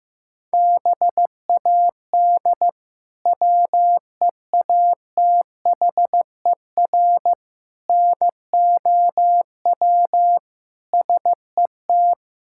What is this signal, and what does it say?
Morse code, heard as a keyed tone, about 16 wpm, 700 Hz. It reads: BAD WEATHER NOW SET
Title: Sentences from Top 1000 Words - Review Part 5 - 15wpm